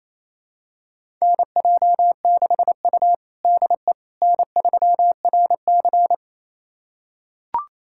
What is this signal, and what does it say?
Morse code, heard as a keyed tone, about 28 wpm, 700 Hz. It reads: NJ6U DE N3RC
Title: USA Callsign DE Callsign - 4 Characters - 28wpm